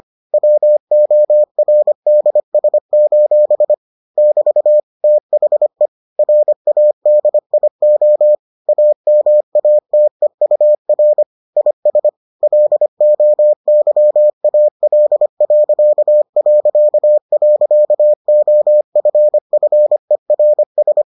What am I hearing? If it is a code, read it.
Morse code, heard as a keyed tone, about 25 wpm, 600 Hz. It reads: WORDS: <BT> THE RADIO AMATEUR IS LOYAL...OFFERS